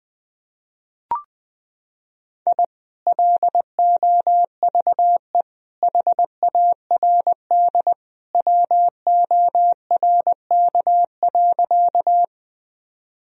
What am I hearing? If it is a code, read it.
Morse code, heard as a keyed tone, about 20 wpm, 700 Hz. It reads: I LOVE HARD WORK.
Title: Sentences from Top 1000 Words - Review Part 1 - 20wpm